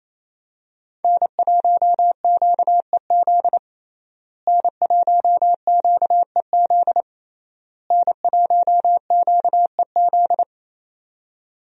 Morse code, heard as a keyed tone, about 28 wpm, 700 Hz. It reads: N1QEZ N1QEZ N1QEZ
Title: USA Call Signs - 5 Characters - Repeated 3x - 28wpm